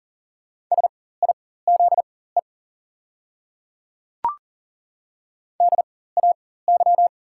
Morse code, heard as a keyed tone, about 40 wpm, spaced as for 19 wpm, 690 Hz. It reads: SIZE DAY
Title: 4x Character Spacing - Top 500 Words - 40wpm